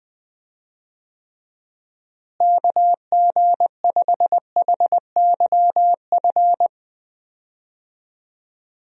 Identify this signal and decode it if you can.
Morse code, heard as a keyed tone, about 21 wpm, 700 Hz. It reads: KG5HYF
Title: USA Call Signs - 6 Characters - Repeated 3x - 20wpm